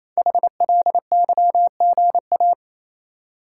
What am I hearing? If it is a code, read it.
Morse code, heard as a keyed tone, about 27 wpm, 700 Hz. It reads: HLYGA
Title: Code Groups - 5 Letters - 28wpm